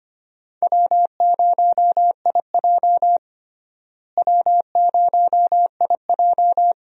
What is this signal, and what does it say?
Morse code, heard as a keyed tone, about 25 wpm, 700 Hz. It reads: W0IJ W0IJ
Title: USA Call Signs - 4 Characters - Repeated 2x - 25wpm